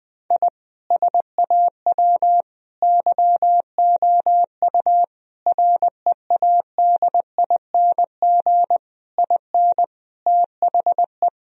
Morse code, heard as a keyed tone, about 20 wpm, 700 Hz. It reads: I SAW YOU READING IN THE
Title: Sentences from Top 1000 Words - Review Part 4 - 20wpm